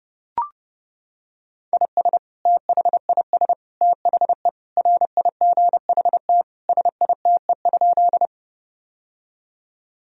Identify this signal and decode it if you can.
Morse code, heard as a keyed tone, about 32 wpm, 700 Hz. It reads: IS THIS THE RIGHT SITE?
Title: Sentences from Top 1000 Words - Review Part 4 - 30wpm